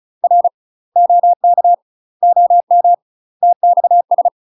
Morse code, heard as a keyed tone, about 34 wpm, 700 Hz. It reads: R OK OM TXS
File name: UK MASTER QSOs TRAINING / 35WPM / UK MASTER QSOs 0005